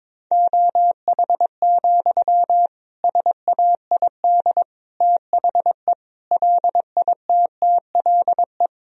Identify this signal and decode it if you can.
Morse code, heard as a keyed tone, about 21 wpm, 700 Hz. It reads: OH, SAID THE LITTLE
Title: Sentences from Top 1000 Words - Review Part 2 - 22wpm